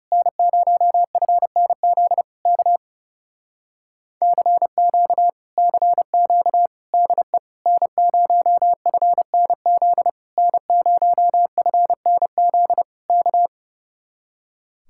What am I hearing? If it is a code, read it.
Morse code, heard as a keyed tone, about 28 wpm, 700 Hz. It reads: N0FNZ K CQ CQ DE N0FNZ N0FNZ K
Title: Speed-Racing Common QSO Phrases - 30wpm